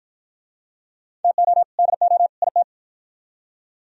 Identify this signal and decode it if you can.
Morse code, heard as a keyed tone, about 46 wpm, 700 Hz. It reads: TO DO IT